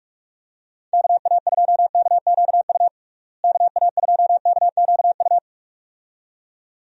Morse code, heard as a keyed tone, about 46 wpm, 700 Hz. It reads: KA2KQU KA2KQU
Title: USA Call Signs - 6 Characters - Repeated 2x - 45wpm